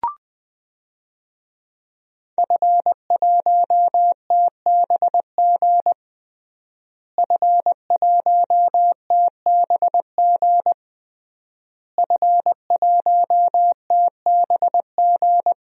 Morse code, heard as a keyed tone, about 20 wpm, 700 Hz. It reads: F1TBG F1TBG F1TBG
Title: DXCC-100 Call Signs - 5 Characters - Repeated 3x - 20wpm